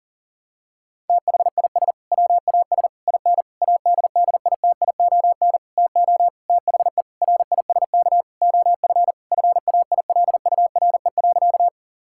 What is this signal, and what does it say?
Morse code, heard as a keyed tone, about 37 wpm, 700 Hz. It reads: THIS WAS IN ADDITION TO THE RISK OF FAILURE.
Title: Sentences from Top 1000 Words - Review Part 6 - 40wpm